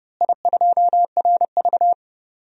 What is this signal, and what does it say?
Morse code, heard as a keyed tone, about 30 wpm, 700 Hz. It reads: I2RV
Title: DXCC-100 Call Signs - 4 Characters - Repeated 2x - 30wpm